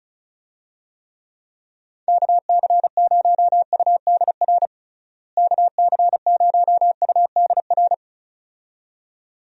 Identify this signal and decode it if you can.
Morse code, heard as a keyed tone, about 36 wpm, 700 Hz. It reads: KC0UDR KC0UDR
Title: USA Call Signs - 6 Characters - Repeated 2x - 35wpm